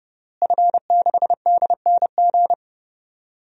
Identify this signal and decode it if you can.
Morse code, heard as a keyed tone, about 32 wpm, 700 Hz. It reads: F6DNG
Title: DXCC-100 Call Signs - 5 Characters - Repeated 3x - 30wpm